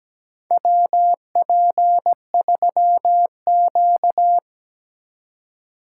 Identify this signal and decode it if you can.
Morse code, heard as a keyed tone, about 17 wpm, 700 Hz. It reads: WP3Q
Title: USA Call Signs - 4 Characters - 4x Character Spacing - 17wpm